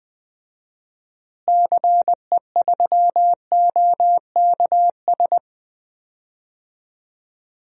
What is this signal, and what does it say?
Morse code, heard as a keyed tone, about 20 wpm, 700 Hz. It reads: CE3OKS
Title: DXCC-100 Call Signs - 6 Characters - Repeated 2x - 20wpm